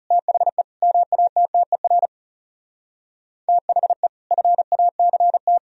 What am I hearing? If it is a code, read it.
Morse code, heard as a keyed tone, about 40 wpm, 700 Hz. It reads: THE MATTER THE FACT